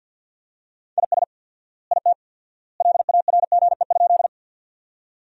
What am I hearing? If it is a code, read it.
Morse code, heard as a keyed tone, about 49 wpm, 700 Hz. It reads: IS IT LARGE?
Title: 4x Word-Spacing - Sentences from Top 200 Words - 50wpm